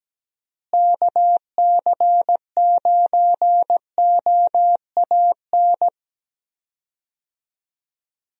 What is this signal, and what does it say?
Morse code, heard as a keyed tone, about 17 wpm, 700 Hz. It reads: KC9OAN
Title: USA Call Signs - 6 Characters - Repeated 3x - 17wpm